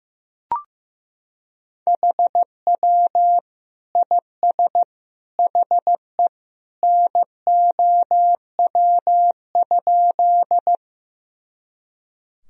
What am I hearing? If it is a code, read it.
Morse code, heard as a keyed tone, about 15 wpm, 700 Hz. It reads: HW IS HE NOW?
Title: Sentences from Top 100 Words and Abbreviations 15wpm